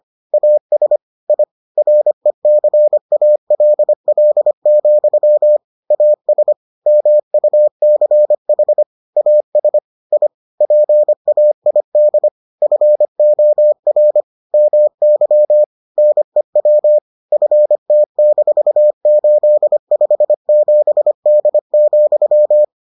Morse code, heard as a keyed tone, about 25 wpm, 600 Hz. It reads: AS I RECALL, AS MUCH AS I PAID FOR MY NEW FT-857D,